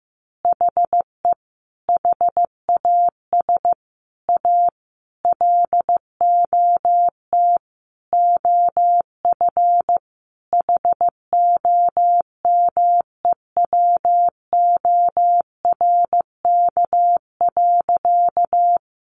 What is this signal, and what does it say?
Morse code, heard as a keyed tone, about 15 wpm, 700 Hz. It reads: HE HAS A LOT OF HOMEWORK.
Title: Additional Easy Phrases 15wpm